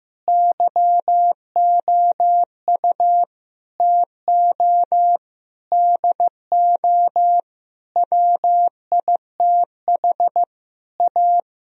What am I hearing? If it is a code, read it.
Morse code, heard as a keyed tone, about 15 wpm, 700 Hz. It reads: YOU TO DO WITH A
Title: Sentences from Top 1000 Words - Review Part 2 - 15wpm